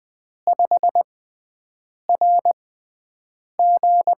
Morse code, heard as a keyed tone, about 20 wpm, spaced as for 7 wpm, 700 Hz. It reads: 5RG
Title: USA Call Signs - 4 Characters - 6x Character Spacing - 20wpm